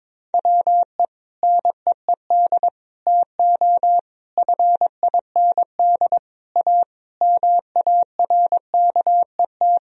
Morse code, heard as a keyed tone, about 22 wpm, 700 Hz. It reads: WE NEED TO FIND A MARKET
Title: Sentences from Top 1000 Words - Review Part 3 - 22wpm